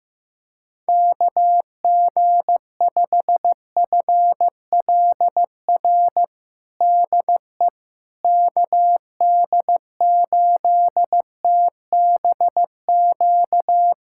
Morse code, heard as a keyed tone, about 15 wpm, 700 Hz. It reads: KG5FLR DE KD8TBQ
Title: USA Callsign DE Callsign 15wpm